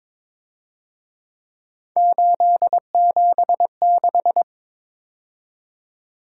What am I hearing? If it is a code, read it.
Morse code, heard as a keyed tone, about 26 wpm, 700 Hz. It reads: 876
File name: Speed-Racing - Three Letter-Number Combinations - 22wpm